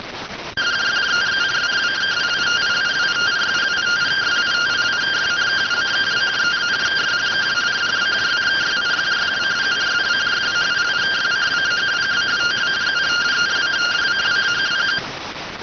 Two-tone teleprinter signal, about 170 Hz shift.